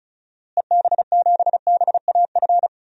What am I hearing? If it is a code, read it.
Morse code, heard as a keyed tone, about 34 wpm, 700 Hz. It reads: EB7BAF
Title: DXCC-100 Call Signs - 6 Characters - Repeated 3x - 35wpm